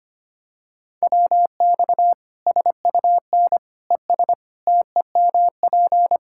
Morse code, heard as a keyed tone, about 25 wpm, 700 Hz. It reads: WX SUN ES TEMP
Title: Practice QSO - Exchange 2 - Info - 25wpm